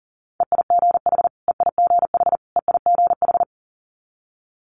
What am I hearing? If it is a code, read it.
Morse code, heard as a keyed tone, about 41 wpm, 700 Hz. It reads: EIGH EIGH EIGH
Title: Multi-letter Phonemes Repeated 3x 40wpm